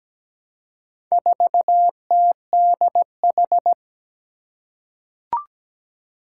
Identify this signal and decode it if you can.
Morse code, heard as a keyed tone, about 17 wpm, 700 Hz. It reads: 4TDH
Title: Four Challenging Letter-Number Combinations 17wpm